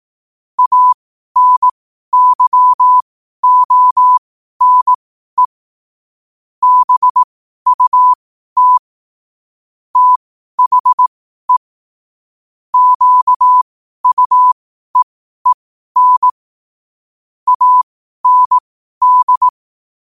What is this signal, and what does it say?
Morse code, heard as a keyed tone, about 18 wpm, spaced as for 12 wpm, 1000 Hz. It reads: ANYONE BUT THE QUEEN AND
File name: qotdfiles / 12wpm